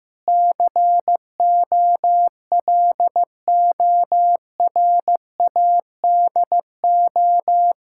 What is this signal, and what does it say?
Morse code, heard as a keyed tone, about 15 wpm, 700 Hz. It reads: COLORADO
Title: Practice QSO - Exchange 1 - Beginning - 15wpm